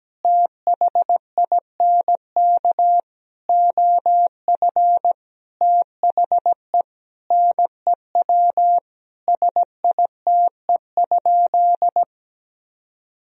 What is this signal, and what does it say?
Morse code, heard as a keyed tone, about 17 wpm, 700 Hz. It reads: THINK OF THE NEW SITE?
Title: Sentences from Top 1000 Words - Review Part 4 - 17wpm